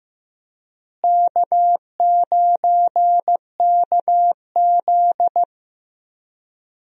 Morse code, heard as a keyed tone, about 15 wpm, 700 Hz. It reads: K9KZ
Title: USA Call Signs - 4 Characters - Repeated 3x - 15wpm